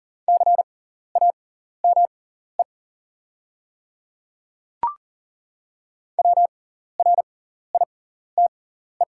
Morse code, heard as a keyed tone, about 40 wpm, spaced as for 14 wpm, 700 Hz. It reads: CAME WRITE
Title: 6x Character Spacing - Top 500 Words - 40wpm